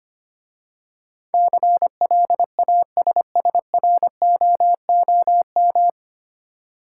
Morse code, heard as a keyed tone, about 25 wpm, 690 Hz. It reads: CLASSROOM
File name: practice / 25wpm / 9 Letter Words - First 3 Letters Given - 25wpm